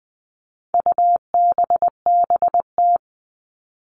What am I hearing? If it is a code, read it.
Morse code, heard as a keyed tone, about 20 wpm, 700 Hz. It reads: UBBT
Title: All Cut Numbers - 4-Digit 20wpm